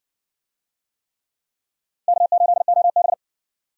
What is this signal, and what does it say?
Morse code, heard as a keyed tone, about 60 wpm, 700 Hz. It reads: K8OB